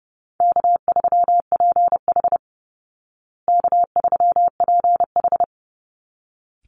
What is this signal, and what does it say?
Morse code, heard as a keyed tone, about 30 wpm, 700 Hz. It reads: K3PH K3PH
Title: Easy Call Signs Repeated 2x - 30wpm